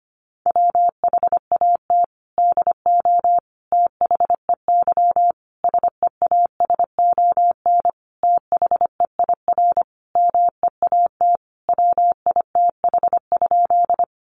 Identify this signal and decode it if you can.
Morse code, heard as a keyed tone, about 25 wpm, 700 Hz. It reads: WHAT DO THEY SEASON THEIR MEAT WITH?
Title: Sentences from Top 1400 Words 25wpm